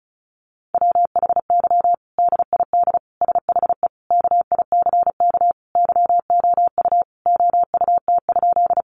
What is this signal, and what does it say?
Morse code, heard as a keyed tone, about 34 wpm, 700 Hz. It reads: WHY DID SHE KICK YOU OUT?
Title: Sentences from Top 1300 Words 35wpm